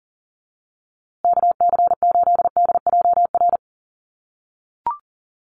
Morse code, heard as a keyed tone, about 40 wpm, 700 Hz. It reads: KC8DJR
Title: USA Call Signs Repeated 2x 40wpm